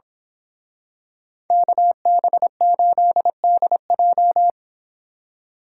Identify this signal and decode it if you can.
Morse code, heard as a keyed tone, about 26 wpm, 700 Hz. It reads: KB8DJ
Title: Speed Racing - USA 2x2 Call Signs 22wpm